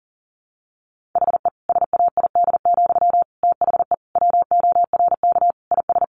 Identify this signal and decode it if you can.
Morse code, heard as a keyed tone, about 40 wpm, 700 Hz. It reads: HE SAID, THE WORK IS
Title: Sentences from Top 400 Words 40wpm